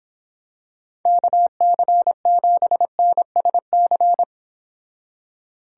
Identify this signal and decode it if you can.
Morse code, heard as a keyed tone, about 26 wpm, 700 Hz. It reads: KC7NSC